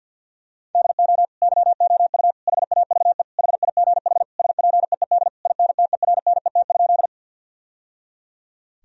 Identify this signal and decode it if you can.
Morse code, heard as a keyed tone, about 50 wpm, 700 Hz. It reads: DO YOU HAVE HIGH SPEED INTERNET?
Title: Sentences from Top 1000 Words - Review Part 4 - 50wpm